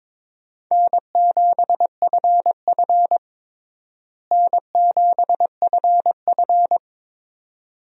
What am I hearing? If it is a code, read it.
Morse code, heard as a keyed tone, about 22 wpm, 700 Hz. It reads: N7FF N7FF
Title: USA Call Signs - 4 Characters - Repeated 2x - 22wpm